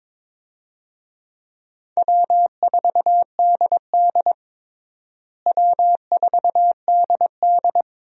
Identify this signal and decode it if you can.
Morse code, heard as a keyed tone, about 22 wpm, 700 Hz. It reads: W4DD W4DD
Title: USA Call Signs - 4 Characters - Repeated 3x - 22wpm